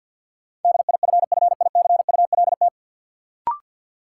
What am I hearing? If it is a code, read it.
Morse code, heard as a keyed tone, about 50 wpm, 700 Hz. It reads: DIFFICULT
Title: Sentences from Top 600 Words - New Word Drill - 50wpm